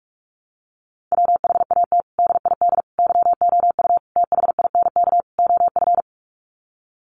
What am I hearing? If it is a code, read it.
Morse code, heard as a keyed tone, about 44 wpm, 700 Hz. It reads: WHAT DID YOU THINK OF
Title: Sets of 5 Words Encore2 45wpm